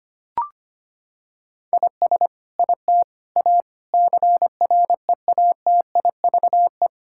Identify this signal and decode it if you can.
Morse code, heard as a keyed tone, about 25 wpm, 700 Hz. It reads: IS IT A CREATIVE
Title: Sentences from Top 1000 Words - Review Part 5 - 25wpm